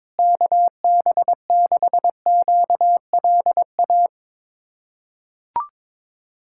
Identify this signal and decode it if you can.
Morse code, heard as a keyed tone, about 22 wpm, 700 Hz. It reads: KB6QLA
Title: USA Callsign DE Callsign 22wpm